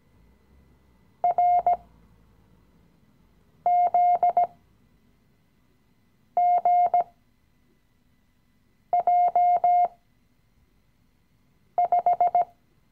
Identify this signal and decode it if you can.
Morse code, heard as a keyed tone, about 18 wpm, spaced as for 4 wpm, 710 Hz. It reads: RZGJ5
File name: CW-Online-Kurs / Anfänger / Lektion 016